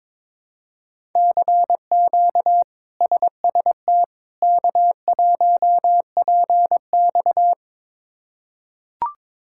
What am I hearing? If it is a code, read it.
Morse code, heard as a keyed tone, about 21 wpm, 700 Hz. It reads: CQ SST K1PX
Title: K1USN SST QSO Practice - 22wpm